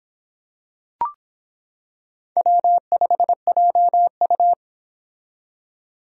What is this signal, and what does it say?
Morse code, heard as a keyed tone, about 25 wpm, 700 Hz. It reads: W5JU